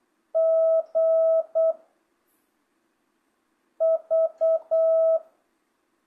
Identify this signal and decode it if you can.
Morse code, heard as a keyed tone, about 8 wpm, 660 Hz. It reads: G V
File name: jh8cbh / 2025cw